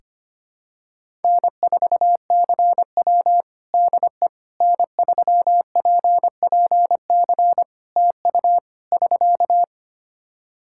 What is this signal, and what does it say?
Morse code, heard as a keyed tone, about 25 wpm, 700 Hz. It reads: N4CW DE N3PPC TU <SK>
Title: Practice QSO - Exchange 4 - Ending - 25wpm